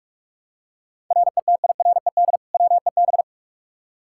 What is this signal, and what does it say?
Morse code, heard as a keyed tone, about 44 wpm, 700 Hz. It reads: RETIRED WEB
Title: K1USN SST QSO Practice 45wpm